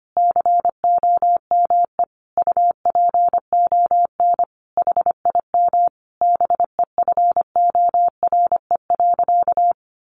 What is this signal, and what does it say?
Morse code, heard as a keyed tone, about 25 wpm, 700 Hz. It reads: COME UPON HIM BEFORE.
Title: Sentences from Top 1100 Words 25wpm